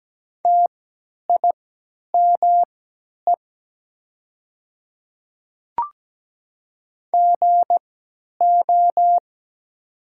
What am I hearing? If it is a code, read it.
Morse code, heard as a keyed tone, about 17 wpm, spaced as for 10 wpm, 700 Hz. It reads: TIME GO
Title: 3x Character Spacing - Top 300 Words - 17wpm